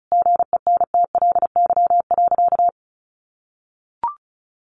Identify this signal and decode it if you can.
Morse code, heard as a keyed tone, about 34 wpm, 700 Hz. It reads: GENTLY.
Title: Sentences from Top 1500 Words 35wpm